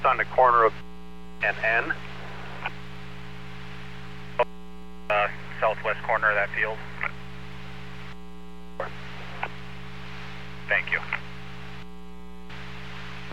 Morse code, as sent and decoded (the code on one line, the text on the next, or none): none